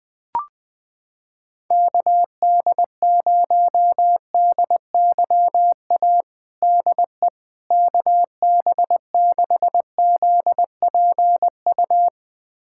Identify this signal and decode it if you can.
Morse code, heard as a keyed tone, about 20 wpm, 700 Hz. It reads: KD0DYA DE KB6ZPU
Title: USA Callsign DE Callsign 20wpm